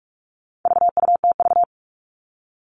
Morse code, heard as a keyed tone, about 44 wpm, 700 Hz. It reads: VUTV
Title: All Cut Numbers - 4-Digit 45wpm